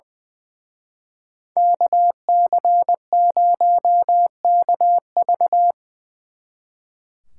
Morse code, heard as a keyed tone, about 23 wpm, 700 Hz. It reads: KC0KV
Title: Speed Racing - USA 2x2 Call Signs 20wpm